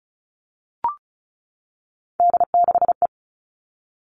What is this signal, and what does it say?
Morse code, heard as a keyed tone, about 36 wpm, 700 Hz. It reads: D6E
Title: Three Challenging Letter-Number Combinations 35wpm